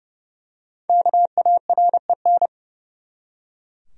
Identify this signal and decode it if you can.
Morse code, heard as a keyed tone, about 30 wpm, 700 Hz. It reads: KAREN
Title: USA Common Names - 30wpm